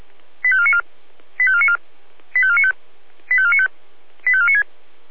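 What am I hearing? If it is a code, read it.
Five-tone selective call (ZVEI), heard as quick runs of five tones, tones of about 70 ms. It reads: 85494 85494 85495 85495 85496
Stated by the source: ZVEI five-tone call 85492 received on an SDR.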